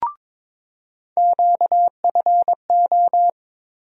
Morse code, heard as a keyed tone, about 22 wpm, 700 Hz. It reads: QFO
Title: USA Call Sign Postfixes - 3 Letters 22wpm